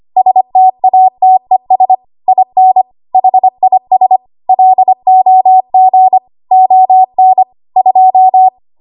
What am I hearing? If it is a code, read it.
Morse code, heard as a keyed tone, about 25 wpm, 750 Hz. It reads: STATES IN HIS LOG ON 2
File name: Morse / 25WPM